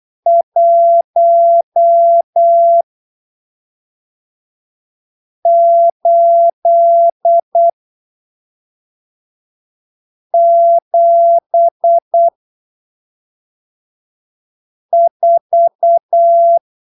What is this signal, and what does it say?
Morse code, heard as a keyed tone, about 8 wpm, spaced as for 3 wpm, 680 Hz. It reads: 1874
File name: arquivos / texto / texto 16 08ppm